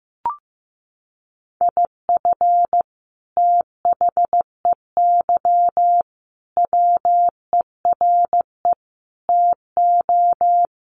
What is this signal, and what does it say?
Morse code, heard as a keyed tone, about 15 wpm, 700 Hz. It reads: IF THEY WERE TO
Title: Sentences from Top 1500 Words 15wpm